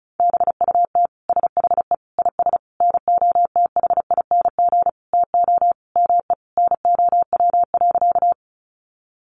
Morse code, heard as a keyed tone, about 34 wpm, 700 Hz. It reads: BUT SHE IS NOTHING TO ME NOW.
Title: Sentences from Top 500 Words 35wpm